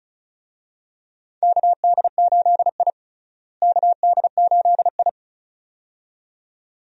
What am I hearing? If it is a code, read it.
Morse code, heard as a keyed tone, about 34 wpm, 700 Hz. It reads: KD8I KD8I
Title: USA Call Signs - 4 Characters - Repeated 2x - 35wpm